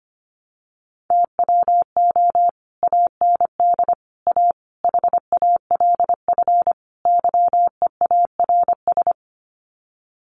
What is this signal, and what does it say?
Morse code, heard as a keyed tone, about 25 wpm, 700 Hz. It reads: TWO AND A HALF YEARS
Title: Sets of 5 Words 25wpm